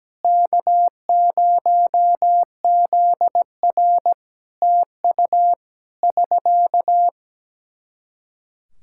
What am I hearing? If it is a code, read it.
Morse code, heard as a keyed tone, about 17 wpm, 700 Hz. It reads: K0ZR TU <SK>
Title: K1USN SST QSO Practice 17wpm